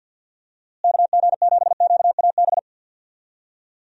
Morse code, heard as a keyed tone, about 50 wpm, 700 Hz. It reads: KG7QAB
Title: USA Call Signs - 6 Characters - Repeated 3x - 50wpm